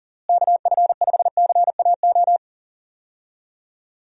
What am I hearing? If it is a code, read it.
Morse code, heard as a keyed tone, about 40 wpm, 700 Hz. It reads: KF5CAO
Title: USA Call Signs - 6 Characters - Repeated 3x - 40wpm